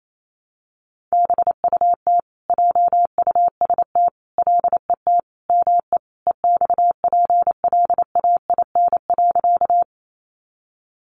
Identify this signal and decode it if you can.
Morse code, heard as a keyed tone, about 28 wpm, 700 Hz. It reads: BUT JUST LET ME EXPLAIN.
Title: Sentences from Top 1100 Words 28wpm